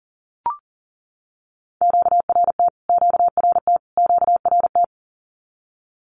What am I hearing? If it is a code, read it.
Morse code, heard as a keyed tone, about 40 wpm, 700 Hz. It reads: QRT QRT QRT
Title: Q-Codes Repeated 3x 40wpm